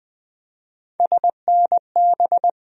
Morse code, heard as a keyed tone, about 21 wpm, 700 Hz. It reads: SNB
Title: USA Call Sign Postfixes - 3 Letters 20wpm